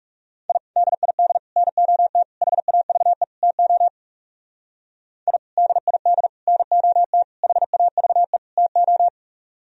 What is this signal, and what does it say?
Morse code, heard as a keyed tone, about 37 wpm, 700 Hz. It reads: I DID NOT HAVE TO I DID NOT HAVE TO
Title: Speed-Racing - Sets of 5 Words 30wpm